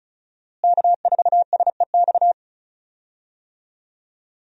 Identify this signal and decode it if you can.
Morse code, heard as a keyed tone, about 34 wpm, 700 Hz. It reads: K4SEX
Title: USA Call Signs - 5 Characters - Repeated 3x - 35wpm